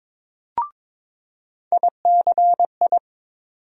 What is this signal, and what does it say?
Morse code, heard as a keyed tone, about 22 wpm, 700 Hz. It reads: ICI
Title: USA Call Sign Postfixes - 3 Letters 22wpm